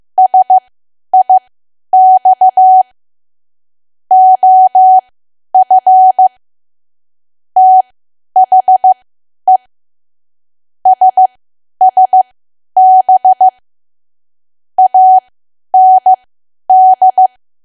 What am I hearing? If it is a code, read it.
Morse code, heard as a keyed tone, about 15 wpm, spaced as for 10 wpm, 750 Hz. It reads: SIX OF THE SSB AND
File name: Archive / 10 WPM / 10WPM